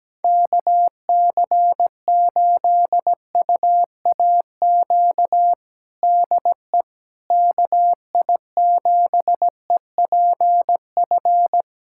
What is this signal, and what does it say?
Morse code, heard as a keyed tone, about 17 wpm, 700 Hz. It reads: KC8UAQ DE KI7EPF
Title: USA Callsign DE Callsign - 6 Characters - 17wpm